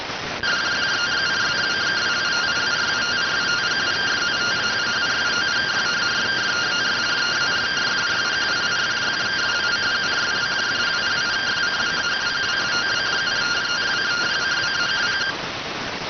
Two-tone teleprinter signal, about 170 Hz shift.